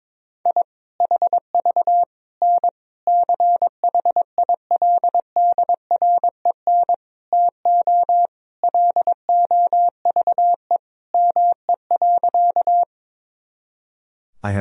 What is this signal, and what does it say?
Morse code, heard as a keyed tone, about 21 wpm, 700 Hz. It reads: I HV N CHILDREN TO LOVE ME.
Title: Sentences from Top 400 Words and Abbreviations 22wpm